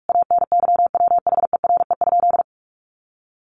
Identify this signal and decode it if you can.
Morse code, heard as a keyed tone, about 44 wpm, 700 Hz. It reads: ANYWHERE?
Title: Sentences from Top 1400 Words 45wpm